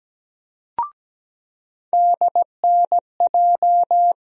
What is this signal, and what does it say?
Morse code, heard as a keyed tone, about 17 wpm, 700 Hz. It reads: DNJ
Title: USA Call Sign Postfixes - 3 Letters 17wpm